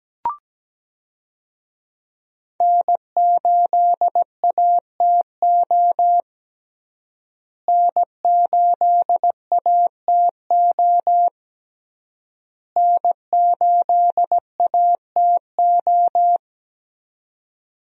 Morse code, heard as a keyed tone, about 17 wpm, 700 Hz. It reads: N8ATO N8ATO N8ATO
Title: USA Call Signs - 5 Characters - Repeated 3x - 17wpm